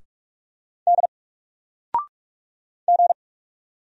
Morse code, heard as a keyed tone, about 44 wpm, 700 Hz. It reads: D G
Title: Single Letter-Number 45wpm